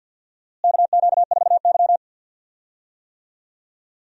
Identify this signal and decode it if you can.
Morse code, heard as a keyed tone, about 50 wpm, 700 Hz. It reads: KQ4Y
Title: USA Call Signs - 4 Characters - Repeated 2x - 50wpm